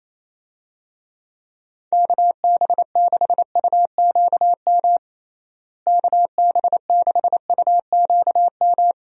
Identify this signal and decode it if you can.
Morse code, heard as a keyed tone, about 28 wpm, 700 Hz. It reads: KB6UQM KB6UQM
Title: USA Call Signs - 6 Characters - Repeated 2x - 28wpm